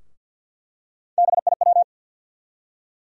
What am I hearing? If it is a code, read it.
Morse code, heard as a keyed tone, about 52 wpm, 700 Hz. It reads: BIW
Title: USA Call Sign Postfixes - 3 Letters 50wpm